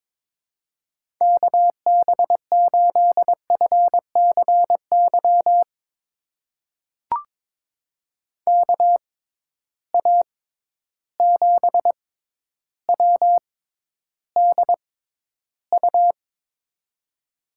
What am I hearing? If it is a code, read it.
Morse code, heard as a keyed tone, about 22 wpm, spaced as for 11 wpm, 700 Hz. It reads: KB8FCY KA7WDU
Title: USA Call Signs - 6 Characters - 6x Character Spacing - 22wpm